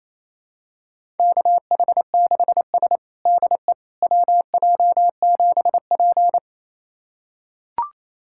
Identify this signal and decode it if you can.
Morse code, heard as a keyed tone, about 28 wpm, 700 Hz. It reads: KH6S DE WJ7P
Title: USA Callsign DE Callsign - 4 Characters - 28wpm